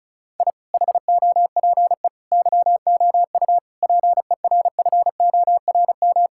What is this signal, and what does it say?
Morse code, heard as a keyed tone, about 36 wpm, 700 Hz. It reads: I HOPE YOU PERFORM
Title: Sentences from Top 1000 Words - Review Part 6 - 35wpm